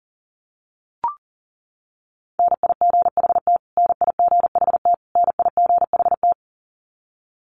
Morse code, heard as a keyed tone, about 40 wpm, 700 Hz. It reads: NIGHT NIGHT NIGHT
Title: Top 500 Words Repeated 40wpm